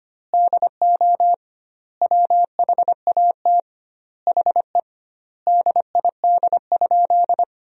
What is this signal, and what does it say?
Morse code, heard as a keyed tone, about 25 wpm, 700 Hz. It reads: DO WHAT HE DID?
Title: 2x Word-Spacing - Sentences from Top 100 Words - 25wpm